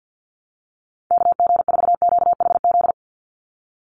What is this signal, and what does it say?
Morse code, heard as a keyed tone, about 50 wpm, 700 Hz. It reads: KG4QHZ
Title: USA Call Signs 50wpm